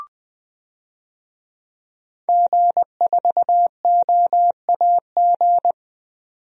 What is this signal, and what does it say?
Morse code, heard as a keyed tone, about 21 wpm, 700 Hz. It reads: G4OAG
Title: DXCC-100 Call Signs - 5 Characters - 20wpm